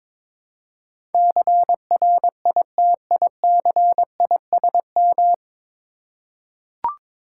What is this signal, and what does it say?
Morse code, heard as a keyed tone, about 22 wpm, 700 Hz. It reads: CRITICISM
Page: Morse Code Ninja practice 22wpm